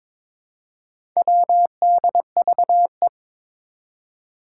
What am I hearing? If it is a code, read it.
Morse code, heard as a keyed tone, about 22 wpm, 700 Hz. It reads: WDVE
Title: Four Challenging Letter-Number Combinations 22wpm